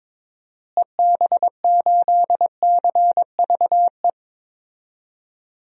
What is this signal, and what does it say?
Morse code, heard as a keyed tone, about 22 wpm, 700 Hz. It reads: EB8CVE
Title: DXCC-100 Call Signs - 6 Characters - Repeated 3x - 22wpm